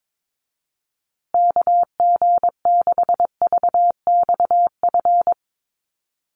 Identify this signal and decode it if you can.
Morse code, heard as a keyed tone, about 22 wpm, 700 Hz. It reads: KG6VXF
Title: USA Call Signs Repeated 2x 22wpm